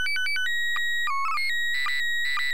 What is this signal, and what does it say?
SSTV (slow-scan television), PD 120.